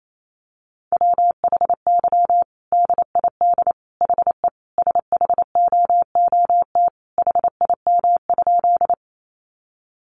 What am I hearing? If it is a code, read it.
Morse code, heard as a keyed tone, about 28 wpm, 700 Hz. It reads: WHY DID HE SHOOT HIM?
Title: Sentences from Top 1200 Words 28wpm